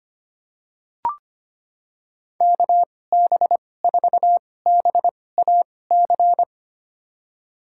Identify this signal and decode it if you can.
Morse code, heard as a keyed tone, about 25 wpm, spaced as for 18 wpm, 700 Hz. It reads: KB4BAC
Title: USA Call Signs - 6 Characters - 2x Character Spacing - 25wpm